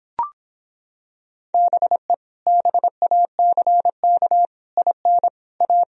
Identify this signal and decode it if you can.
Morse code, heard as a keyed tone, about 26 wpm, 700 Hz. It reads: BE BACK IN A